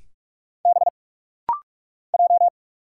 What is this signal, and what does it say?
Morse code, heard as a keyed tone, about 46 wpm, 700 Hz. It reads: B J
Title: Single Letters - Rapid-Fire 45wpm